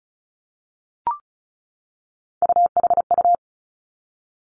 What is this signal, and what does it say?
Morse code, heard as a keyed tone, about 36 wpm, 700 Hz. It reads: UHU
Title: Three Challenging Letter-Number Combinations 35wpm